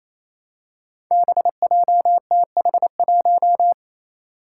Morse code, heard as a keyed tone, about 28 wpm, 700 Hz. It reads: BJTH1
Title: Five Challenging Letter-Number Combinations 28wpm